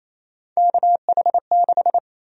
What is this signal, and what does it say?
Morse code, heard as a keyed tone, about 28 wpm, 700 Hz. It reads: KH6
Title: USA Call Sign Prefixes - Repeated 2x - 28wpm